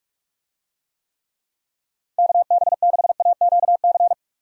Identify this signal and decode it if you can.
Morse code, heard as a keyed tone, about 46 wpm, 700 Hz. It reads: KB6AQC
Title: USA Call Signs - 6 Characters - 45wpm